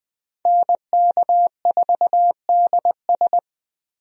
Morse code, heard as a keyed tone, about 20 wpm, 700 Hz. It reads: NK4DS
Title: USA Call Signs - 5 Characters - Repeated 2x - 20wpm